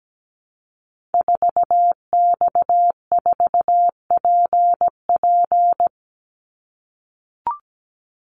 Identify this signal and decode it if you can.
Morse code, heard as a keyed tone, about 17 wpm, 700 Hz. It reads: 4X4PP
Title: Call Signs - 17wpm